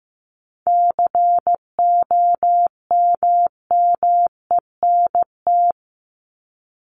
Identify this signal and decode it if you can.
Morse code, heard as a keyed tone, about 15 wpm, 700 Hz. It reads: COMMENT
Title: Top 1751-2000 Words 15wpm